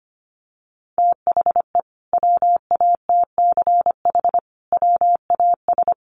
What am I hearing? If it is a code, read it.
Morse code, heard as a keyed tone, about 25 wpm, 700 Hz. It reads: THE WATCH WAS
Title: Sentences from Top 1300 Words 25wpm